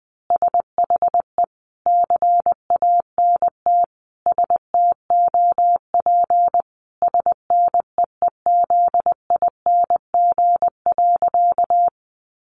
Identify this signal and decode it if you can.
Morse code, heard as a keyed tone, about 20 wpm, 700 Hz. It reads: SHE CANT STOP SNEEZING.
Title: Easy Phrases 20wpm